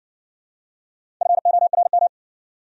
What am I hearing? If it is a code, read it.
Morse code, heard as a keyed tone, about 60 wpm, 700 Hz. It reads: FORM